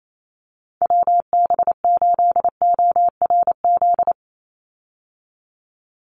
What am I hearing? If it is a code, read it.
Morse code, heard as a keyed tone, about 28 wpm, 700 Hz. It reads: WB8ORZ
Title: USA Call Signs 28wpm